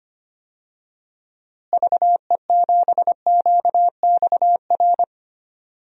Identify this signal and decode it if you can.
Morse code, heard as a keyed tone, about 25 wpm, 700 Hz. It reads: VE7QXR
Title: DXCC-100 Call Signs - 6 Characters - Repeated 2x - 25wpm